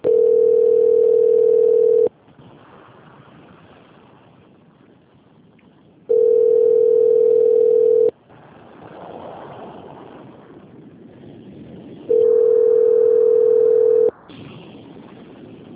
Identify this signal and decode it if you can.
Ringback tone (North America), 440 + 480 Hz.